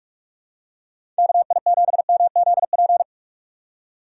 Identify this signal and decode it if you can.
Morse code, heard as a keyed tone, about 44 wpm, 700 Hz. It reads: KI7MZP
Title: USA Call Signs - 6 Characters - 3x Character Spacing - 45wpm